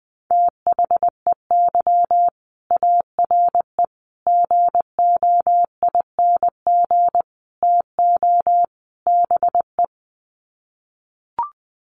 Morse code, heard as a keyed tone, about 19 wpm, 700 Hz. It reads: THEY ARE GOING TO BE
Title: Sets of 5 Words Encore2 20wpm